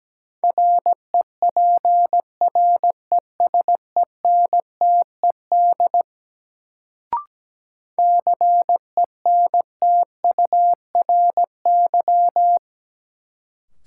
Morse code, heard as a keyed tone, about 17 wpm, 700 Hz. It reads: REPRESENTED CENTURY
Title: Sets of 5 Words - New Word Drill - 17wpm